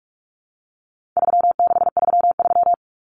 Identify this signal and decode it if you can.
Morse code, heard as a keyed tone, about 44 wpm, 700 Hz. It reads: 3633
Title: Numbers - 4-Digit 45wpm